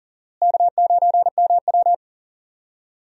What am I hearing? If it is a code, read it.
Morse code, heard as a keyed tone, about 40 wpm, 700 Hz. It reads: K9MW
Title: USA Call Signs - 4 Characters - Repeated 2x - 40wpm